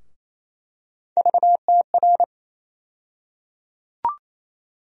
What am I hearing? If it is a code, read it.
Morse code, heard as a keyed tone, about 28 wpm, 700 Hz. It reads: VTR
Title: USA Call Sign Postfixes - 3 Letters 28wpm